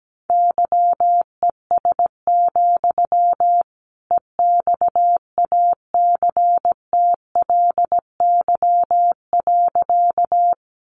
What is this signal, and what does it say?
Morse code, heard as a keyed tone, about 17 wpm, 700 Hz. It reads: YES, EXACTLY.
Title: Sentences from Top 900 Words 17wpm